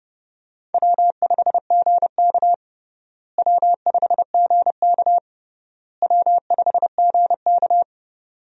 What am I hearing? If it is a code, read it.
Morse code, heard as a keyed tone, about 30 wpm, 700 Hz. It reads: W5GK W5GK W5GK
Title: USA Call Signs - 4 Characters - Repeated 3x - 30wpm